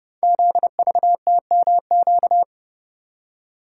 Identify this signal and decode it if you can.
Morse code, heard as a keyed tone, about 29 wpm, 700 Hz. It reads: ZVTMQ
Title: Code Groups - 5 Letters - 30wpm